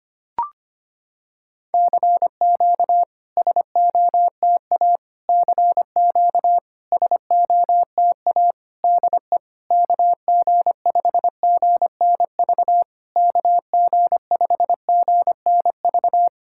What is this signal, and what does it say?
Morse code, heard as a keyed tone, about 25 wpm, 700 Hz. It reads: CQ SOTA CQ SOTA DE KG5GNV KG5GNV
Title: SOTA QSO Practice 25wpm